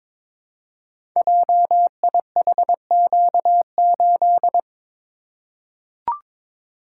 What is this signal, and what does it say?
Morse code, heard as a keyed tone, about 22 wpm, 700 Hz. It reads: JIHQ8
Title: Code Groups - 5 Letter:Numbers - 22wpm